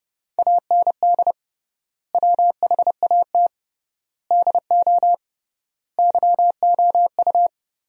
Morse code, heard as a keyed tone, about 30 wpm, 700 Hz. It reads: AND WHAT DO YOU
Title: 3x Word-Spacing - Sentences from Top 200 Words - 30wpm